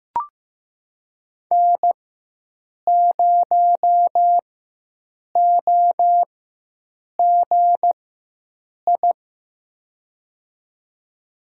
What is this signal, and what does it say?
Morse code, heard as a keyed tone, about 16 wpm, spaced as for 7 wpm, 700 Hz. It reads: N0OGI
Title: USA Call Signs - 5 Characters - 4x Character Spacing - 15wpm